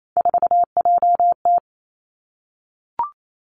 Morse code, heard as a keyed tone, about 28 wpm, 700 Hz. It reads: 4JT
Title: Three Challenging Letter-Number Combinations 28wpm